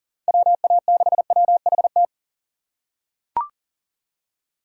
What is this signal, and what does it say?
Morse code, heard as a keyed tone, about 37 wpm, 700 Hz. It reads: WA6WHT
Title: USA Call Signs - 6 Characters - 6x Character Spacing - 40wpm